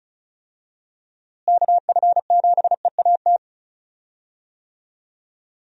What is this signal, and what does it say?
Morse code, heard as a keyed tone, about 34 wpm, 700 Hz. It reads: KF7EAT
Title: USA Call Signs - 6 Characters - Repeated 3x - 35wpm